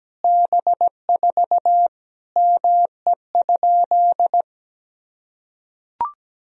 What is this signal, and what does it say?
Morse code, heard as a keyed tone, about 17 wpm, 700 Hz. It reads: B4 ME?
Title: Sentences from Top 200 Words and Abbreviations 17wpm